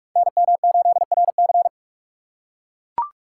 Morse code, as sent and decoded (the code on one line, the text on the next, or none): -. -- ---.. .-. -.-.
NM8RC